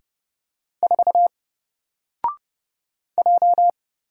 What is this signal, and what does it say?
Morse code, heard as a keyed tone, about 30 wpm, 700 Hz. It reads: 4 J
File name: Single-Letter-Number-30wpm